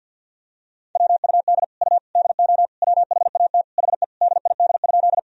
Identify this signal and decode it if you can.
Morse code, heard as a keyed tone, about 52 wpm, 700 Hz. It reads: WUD U DO WHAT HE DID?
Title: Sentences from Top 100 Words and Abbreviations 50wpm